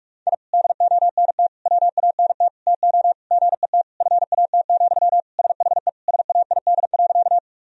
Morse code, heard as a keyed tone, about 46 wpm, 700 Hz. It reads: I DONT WANT TO GET FAT, SHE SAID.
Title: Sentences from Top 1000 Words - Review Part 4 - 45wpm